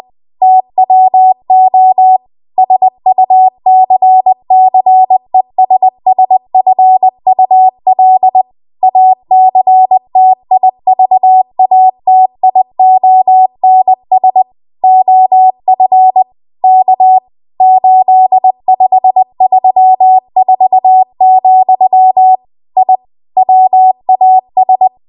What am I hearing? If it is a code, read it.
Morse code, heard as a keyed tone, about 20 wpm, 750 Hz. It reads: TWO SUCCESSFUL ACTIVATIONS OF K 8534, I WAS
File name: Archive / 20 WPM / 20WPM